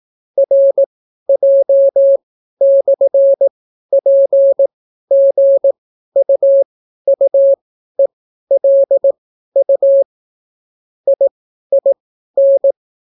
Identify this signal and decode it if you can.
Morse code, heard as a keyed tone, about 18 wpm, spaced as for 12 wpm, 550 Hz. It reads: RJ/PGUUELU IIN